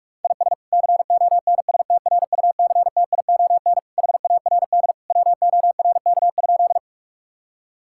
Morse code, heard as a keyed tone, about 43 wpm, 700 Hz. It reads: IS CONSTRUCTION HARD WORK?
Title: Sentences from Top 1600 Words 45wpm